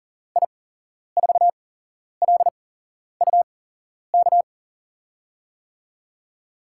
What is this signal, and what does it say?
Morse code, heard as a keyed tone, about 39 wpm, spaced as for 11 wpm, 700 Hz. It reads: I4LUK
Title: USA Call Signs - 8x Character Spacing - 40wpm